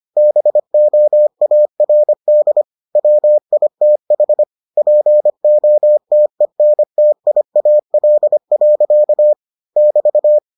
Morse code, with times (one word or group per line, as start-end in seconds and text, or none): 0.2-2.6 BOARD
3.0-4.4 WITH
4.8-9.3 POTENTIAL.
9.8-10.4 <BT>